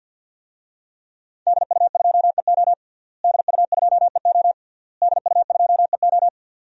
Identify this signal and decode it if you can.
Morse code, heard as a keyed tone, about 53 wpm, 700 Hz. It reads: DU2EO DU2EO DU2EO
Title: DXCC-100 Call Signs - 5 Characters - Repeated 3x - 50wpm